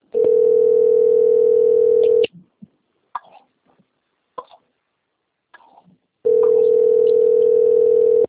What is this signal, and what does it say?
Ringback tone (North America), 440 + 480 Hz.
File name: rg-1600-unknown- - - .26078